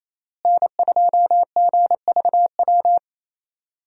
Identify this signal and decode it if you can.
Morse code, heard as a keyed tone, about 28 wpm, 700 Hz. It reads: N2GVW
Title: USA Call Signs - 5 Characters - Repeated 3x - 28wpm